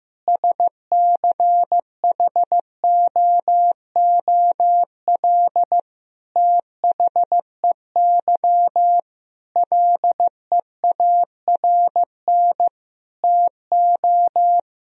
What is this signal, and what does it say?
Morse code, heard as a keyed tone, about 15 wpm, 700 Hz. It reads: SCHOOL THEY LEARN TO
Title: Sentences from Top 1000 Words - Review Part 4 - 15wpm